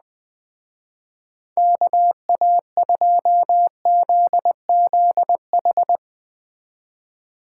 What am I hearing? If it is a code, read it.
Morse code, heard as a keyed tone, about 21 wpm, 700 Hz. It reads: KA2ZZH